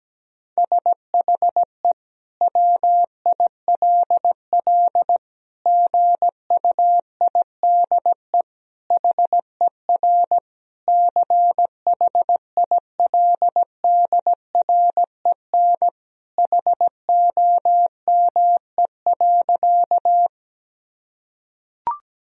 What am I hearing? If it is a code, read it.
Morse code, heard as a keyed tone, about 17 wpm, 700 Hz. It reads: SHE WILL GUIDE HER CHILDREN HOME.
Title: Sentences from Top 1000 Words - Review Part 3 - 17wpm